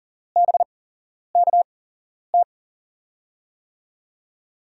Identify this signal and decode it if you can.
Morse code, heard as a keyed tone, about 38 wpm, spaced as for 11 wpm, 700 Hz. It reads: BKT
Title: USA Call Signs - 5 Characters - 8x Character Spacing - 40wpm